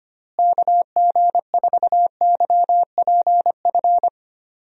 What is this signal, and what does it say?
Morse code, heard as a keyed tone, about 25 wpm, 700 Hz. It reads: KG4YPF
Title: USA Call Signs - 6 Characters - Repeated 3x - 25wpm